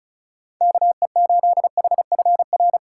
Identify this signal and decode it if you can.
Morse code, heard as a keyed tone, about 34 wpm, 690 Hz. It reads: KE8HFR
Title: USA Call Signs - 6 Characters - 2x Character Spacing - 35wpm